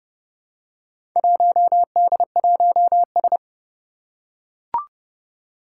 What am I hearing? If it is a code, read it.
Morse code, heard as a keyed tone, about 30 wpm, 700 Hz. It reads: 1D1S
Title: Four Challenging Letter-Number Combinations 30wpm